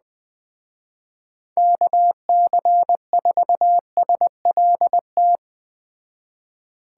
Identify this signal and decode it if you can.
Morse code, heard as a keyed tone, about 19 wpm, 700 Hz. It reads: KC4SLT
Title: Speed Racing - USA 2x3 Call Signs 17wpm